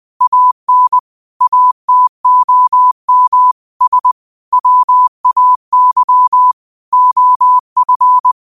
Morse code, heard as a keyed tone, about 20 wpm, 1000 Hz. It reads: AN ATOM S WAY OF